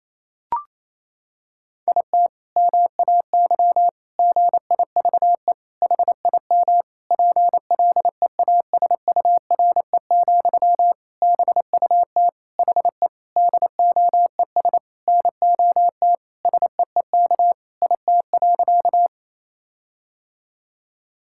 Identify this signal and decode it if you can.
Morse code, heard as a keyed tone, about 28 wpm, 700 Hz. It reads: IT MAY GIVE HIM PLEASURE, BUT HE DOES NOT SEEK IT.
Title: Sentences from Top 1600 Words 28wpm